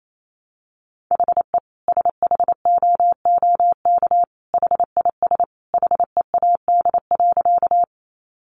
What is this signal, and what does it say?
Morse code, heard as a keyed tone, about 28 wpm, 700 Hz. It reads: HE SHOOK HIS HEAD.
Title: A Sound Of Thunder 28wpm